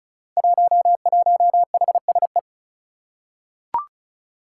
Morse code, heard as a keyed tone, about 36 wpm, 700 Hz. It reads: 11HSE
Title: Five Challenging Letter-Number Combinations 35wpm